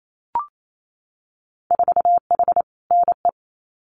Morse code, heard as a keyed tone, about 28 wpm, 700 Hz. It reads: 4H NE
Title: Winter Field Day Exchanges Encore 28wpm